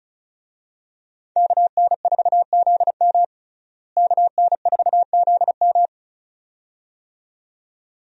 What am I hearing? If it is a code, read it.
Morse code, heard as a keyed tone, about 34 wpm, 700 Hz. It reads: KN4ZM KN4ZM
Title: USA Call Signs - 5 Characters - Repeated 2x - 35wpm